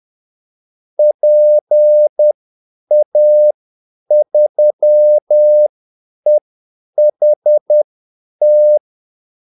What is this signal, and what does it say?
Morse code, heard as a keyed tone, about 10 wpm, spaced as for 8 wpm, 600 Hz. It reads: PA3EHT
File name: Callsigns-10-8wpm